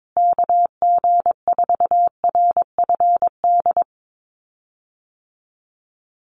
Morse code, heard as a keyed tone, about 22 wpm, 700 Hz. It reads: KG4RFD
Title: USA Call Signs Repeated 3x 22wpm